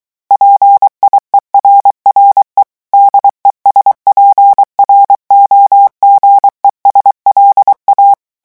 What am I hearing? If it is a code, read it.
Morse code, heard as a keyed tone, about 23 wpm, 800 Hz. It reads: PIERRE DESPROGES LA
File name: morse mp3 / revision9 citations humoristiques vitesse 22 mots minute